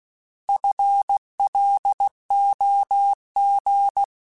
Morse code, heard as a keyed tone, about 16 wpm, 800 Hz. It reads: FLOG
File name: FL Runs / FL Runs mp3s / FL Runs @15